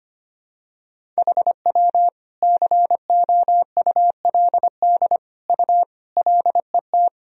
Morse code, heard as a keyed tone, about 25 wpm, 700 Hz. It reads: HW COULD U LET
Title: Sentences from Top 300 Words and Abbreviations 25wpm